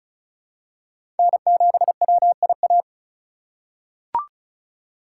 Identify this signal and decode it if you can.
Morse code, heard as a keyed tone, about 34 wpm, 700 Hz. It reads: N7WIA
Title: USA Call Signs - 5 Characters - 6x Character Spacing - 35wpm